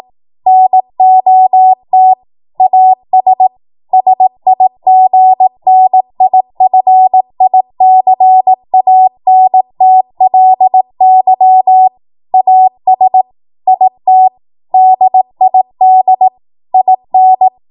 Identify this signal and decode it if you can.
Morse code, heard as a keyed tone, about 18 wpm, 750 Hz. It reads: NOT AS SIGNIFICANTLY AS IT DID IN
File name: Morse / 18WPM